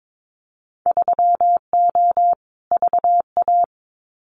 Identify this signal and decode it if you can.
Morse code, heard as a keyed tone, about 22 wpm, 700 Hz. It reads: 3O VA
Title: Winter Field Day Exchanges 22wpm